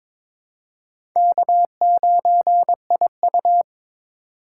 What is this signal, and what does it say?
Morse code, heard as a keyed tone, about 22 wpm, 700 Hz. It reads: K9IU
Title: USA Call Signs - 4 Characters - 4x Character Spacing - 22wpm